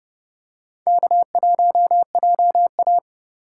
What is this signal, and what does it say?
Morse code, heard as a keyed tone, about 30 wpm, 700 Hz. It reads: K1JA
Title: USA Call Signs - 4 Characters - 2x Character Spacing - 30wpm